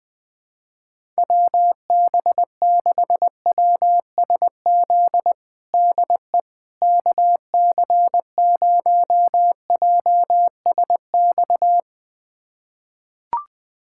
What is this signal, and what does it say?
Morse code, heard as a keyed tone, about 20 wpm, 700 Hz. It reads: WB6WSZ DE KC0JSX
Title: USA Callsign DE Callsign 20wpm